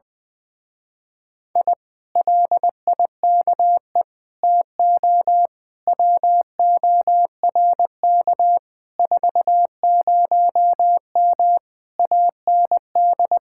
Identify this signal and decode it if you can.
Morse code, heard as a keyed tone, about 20 wpm, 700 Hz. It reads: I LIKE TO WORK 40M AND
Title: Speed-Racing Common QSO Phrases - 20wpm